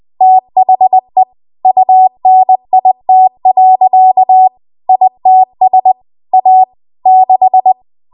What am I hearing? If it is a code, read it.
Morse code, heard as a keyed tone, about 20 wpm, 750 Hz. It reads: THE UNIT. ITS A 6
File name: Morse / 20WPM